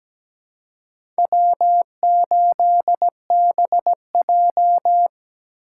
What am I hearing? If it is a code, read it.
Morse code, heard as a keyed tone, about 17 wpm, 700 Hz. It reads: W8BJ
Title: USA Call Signs - 4 Characters - 2x Character Spacing - 17wpm